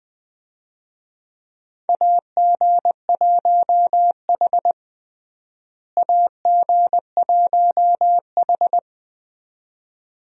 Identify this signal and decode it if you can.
Morse code, heard as a keyed tone, about 20 wpm, 700 Hz. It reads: AG1H AG1H
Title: USA Call Signs - 4 Characters - Repeated 2x - 20wpm